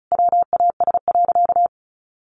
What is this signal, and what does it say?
Morse code, heard as a keyed tone, about 34 wpm, 700 Hz. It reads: WAS.
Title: Sentences from Top 900 Words 35wpm